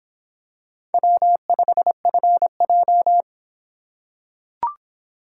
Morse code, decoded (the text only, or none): W5FJ